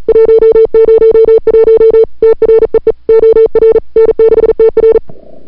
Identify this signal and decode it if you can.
Morse code, heard as a keyed tone, about 40 wpm, 440 Hz. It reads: 101 TREE OR N6TR
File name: n6tr / Fab2007CWSprintHour1